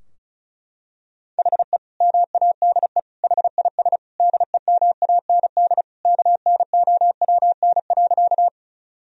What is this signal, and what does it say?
Morse code, heard as a keyed tone, about 34 wpm, 700 Hz. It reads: HE MADE HIS DEMAND KNOWN.
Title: Sentences from Top 1000 Words - Review Part 4 - 35wpm